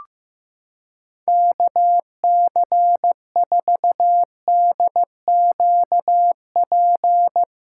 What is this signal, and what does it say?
Morse code, heard as a keyed tone, about 15 wpm, 700 Hz. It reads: KC4DQP
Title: USA Callsign DE Callsign 15wpm